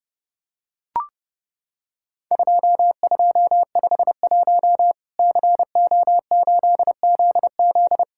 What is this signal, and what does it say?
Morse code, heard as a keyed tone, about 29 wpm, 700 Hz. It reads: 2251 CO8ZZ
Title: Worked All Europe DX Contest - QTC Exchange - 30wpm